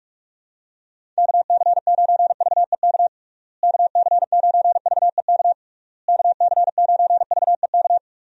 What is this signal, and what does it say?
Morse code, heard as a keyed tone, about 46 wpm, 700 Hz. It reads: KC9VEK KC9VEK KC9VEK
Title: USA Call Signs - 6 Characters - Repeated 3x - 45wpm